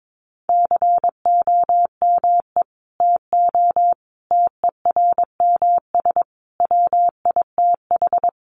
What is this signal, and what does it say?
Morse code, heard as a keyed tone, about 22 wpm, 700 Hz. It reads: COME TO TERMS WITH
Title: Sets of 5 Words Encore2 22wpm